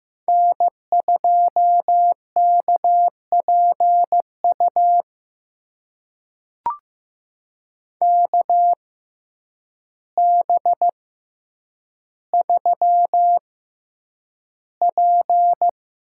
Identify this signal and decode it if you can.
Morse code, heard as a keyed tone, about 15 wpm, spaced as for 8 wpm, 700 Hz. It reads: N2KPU KB3P
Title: USA Call Signs - 6x Character Spacing - 15wpm